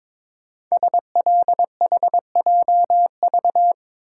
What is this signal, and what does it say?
Morse code, heard as a keyed tone, about 22 wpm, 700 Hz. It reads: SLHJV
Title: Code Groups - 5 Letters - 22wpm